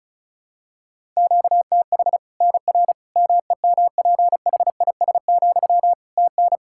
Morse code, heard as a keyed tone, about 36 wpm, 700 Hz. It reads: QTH NR MEMPHIS, TN
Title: Practice QSO - Exchange 1 - Beginning - 35wpm